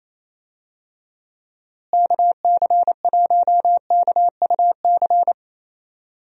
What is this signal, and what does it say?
Morse code, heard as a keyed tone, about 27 wpm, 700 Hz. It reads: KC1KUC
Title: USA Call Signs - 6 Characters - Repeated 3x - 28wpm